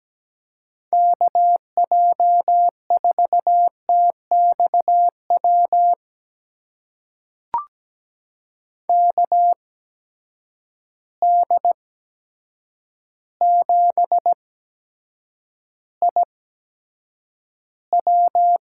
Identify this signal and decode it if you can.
Morse code, heard as a keyed tone, about 17 wpm, spaced as for 8 wpm, 700 Hz. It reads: KJ4TXW KD7IW
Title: USA Call Signs - 6 Characters - 8x Character Spacing - 17wpm